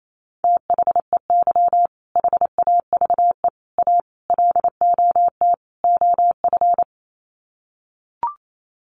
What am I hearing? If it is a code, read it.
Morse code, heard as a keyed tone, about 28 wpm, 700 Hz. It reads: THEY HAVE A LOT OF
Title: Sets of 5 Words Encore2 28wpm